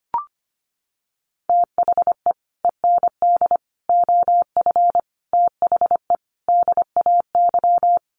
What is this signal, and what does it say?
Morse code, heard as a keyed tone, about 25 wpm, 700 Hz. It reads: THE END OF THE DAY
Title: Sets of 5 Words 25wpm